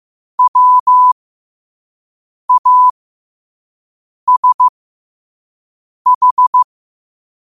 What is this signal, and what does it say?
Morse code, heard as a keyed tone, about 15 wpm, spaced as for 5 wpm, 1000 Hz. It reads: WASH
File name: qotdfiles / 5wpm